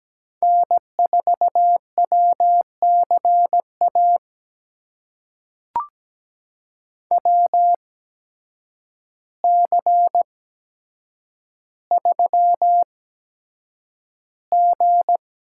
Morse code, heard as a keyed tone, about 17 wpm, spaced as for 8 wpm, 700 Hz. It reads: N4WCA WC3G
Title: USA Call Signs - 8x Character Spacing - 17wpm